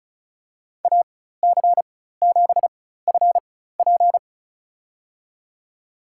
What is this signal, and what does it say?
Morse code, heard as a keyed tone, about 36 wpm, spaced as for 16 wpm, 700 Hz. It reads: AC7FP
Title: USA Call Signs - 5 Characters - 4x Character Spacing - 35wpm